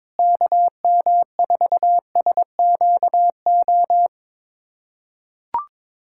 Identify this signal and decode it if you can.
Morse code, heard as a keyed tone, about 22 wpm, 700 Hz. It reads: KM4SQO
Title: USA Callsign DE Callsign 22wpm